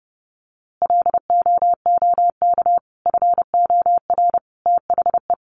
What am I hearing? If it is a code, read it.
Morse code, heard as a keyed tone, about 30 wpm, 700 Hz. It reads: LOOK FOR THE
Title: A Sound Of Thunder 30wpm